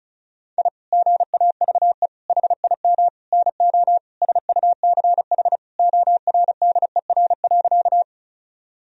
Morse code, heard as a keyed tone, about 36 wpm, 700 Hz. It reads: I GAVE HIM NO SUCH ORDER.
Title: Sentences from Top 1000 Words - Review Part 2 - 35wpm